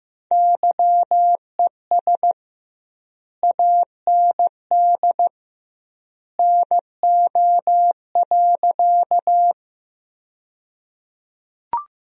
Morse code, heard as a keyed tone, about 16 wpm, 700 Hz. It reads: YES AND NO.
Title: 2x Word-Spacing - Sentences from Top 500 Words - 15wpm